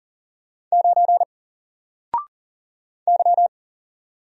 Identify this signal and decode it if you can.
Morse code, heard as a keyed tone, about 40 wpm, 700 Hz. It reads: 9 Y
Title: Single Letter-Number 40wpm